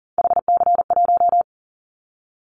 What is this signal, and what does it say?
Morse code, heard as a keyed tone, about 43 wpm, 700 Hz. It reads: HC1
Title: Callsign Prefixes for Top 100 DXCC - 40wpm